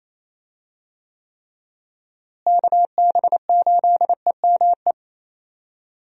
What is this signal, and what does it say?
Morse code, heard as a keyed tone, about 28 wpm, 700 Hz. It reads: KB8EME
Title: USA Call Signs - 6 Characters - Repeated 3x - 28wpm